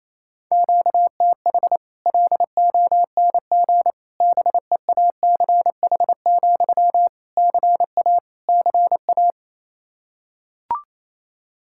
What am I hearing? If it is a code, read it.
Morse code, heard as a keyed tone, about 28 wpm, 700 Hz. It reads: QTH LONG BEACH, CA CA
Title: Practice QSO - Exchange 1 - Beginning - 28wpm